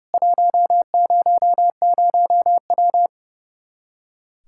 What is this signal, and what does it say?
Morse code, heard as a keyed tone, about 32 wpm, 700 Hz. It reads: 100W
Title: Practice QSO - Exchange 2 - Info - 30wpm